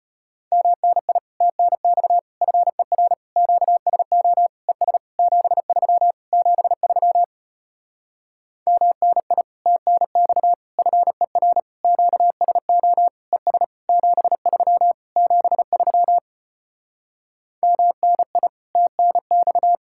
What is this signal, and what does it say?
Morse code, heard as a keyed tone, about 33 wpm, 700 Hz. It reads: MNI TNX FER QSO ES 73 73 MNI TNX FER QSO ES 73 73 MNI TNX